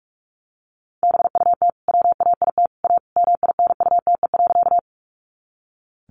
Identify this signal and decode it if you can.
Morse code, heard as a keyed tone, about 44 wpm, 700 Hz. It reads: BUT WAIT A MINUTE.
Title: Sentences from Top 500 Words 45wpm